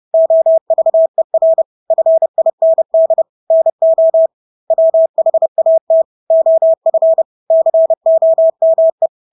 Morse code, heard as a keyed tone, about 30 wpm, 650 Hz. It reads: OVER FIND NO WHAT OF COME
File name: Engelske ord 30-30 wpm